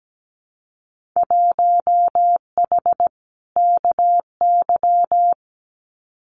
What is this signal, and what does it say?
Morse code, heard as a keyed tone, about 17 wpm, 700 Hz. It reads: 1H KY
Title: Winter Field Day Exchanges 17wpm